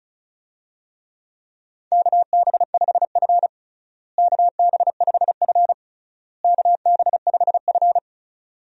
Morse code, heard as a keyed tone, about 36 wpm, 700 Hz. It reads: KB5F KB5F KB5F
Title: USA Call Signs - 4 Characters - Repeated 3x - 35wpm